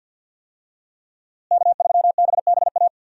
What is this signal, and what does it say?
Morse code, heard as a keyed tone, about 46 wpm, 700 Hz. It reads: K3BBA
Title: USA Call Signs - 5 Characters - Repeated 3x - 50wpm